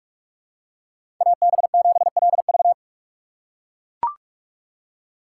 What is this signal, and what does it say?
Morse code, heard as a keyed tone, about 41 wpm, 700 Hz. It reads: AB7LV
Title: USA Call Signs - 5 Characters - 2x Character Spacing - 45wpm